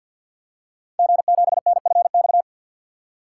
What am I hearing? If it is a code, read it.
Morse code, heard as a keyed tone, about 48 wpm, 700 Hz. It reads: G7NFX
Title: Code Groups - 5 Letter:Numbers - 50wpm